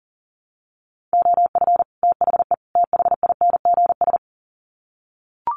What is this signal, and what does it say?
Morse code, heard as a keyed tone, about 40 wpm, 700 Hz. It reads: OF THE THINGS
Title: Sets of 3 Words 40wpm